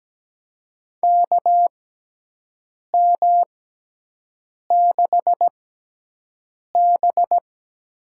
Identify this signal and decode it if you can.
Morse code, heard as a keyed tone, about 17 wpm, spaced as for 6 wpm, 700 Hz. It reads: KM6B
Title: USA Call Signs - 6 Characters - 6x Character Spacing - 17wpm